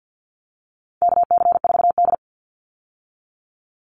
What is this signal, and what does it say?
Morse code, heard as a keyed tone, about 48 wpm, 700 Hz. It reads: KC4D
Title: Call Signs Repeated 3x - 50wpm